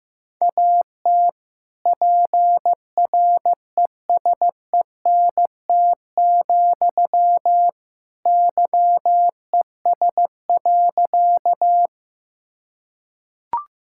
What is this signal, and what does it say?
Morse code, heard as a keyed tone, about 15 wpm, 700 Hz. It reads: AT PRESENT, YES.
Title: Sentences from Top 1000 Words - Review Part 5 - 15wpm